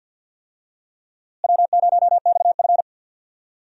Morse code, heard as a keyed tone, about 51 wpm, 700 Hz. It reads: W0XF
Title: USA Call Signs - 4 Characters - Repeated 3x - 50wpm